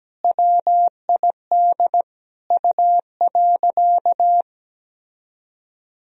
Morse code, heard as a keyed tone, about 17 wpm, 700 Hz. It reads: WID U.
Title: Sentences from Top 300 Words and Abbreviations 17wpm